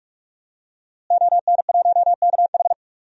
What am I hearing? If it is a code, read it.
Morse code, heard as a keyed tone, about 44 wpm, 700 Hz. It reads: ON1KH
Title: DXCC-100 Call Signs - 5 Characters - Repeated 3x - 45wpm